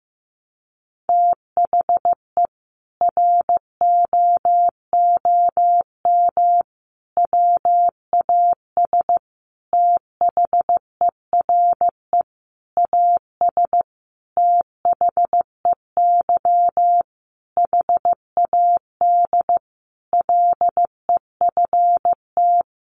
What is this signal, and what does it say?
Morse code, heard as a keyed tone, about 15 wpm, 700 Hz. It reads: THE ROOM WAS THERE AS THEY HAD LEFT
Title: A Sound Of Thunder 15wpm